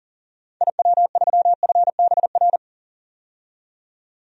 Morse code, heard as a keyed tone, about 38 wpm, 700 Hz. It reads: IW3FBR
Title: DXCC-100 Call Signs - 6 Characters - 40wpm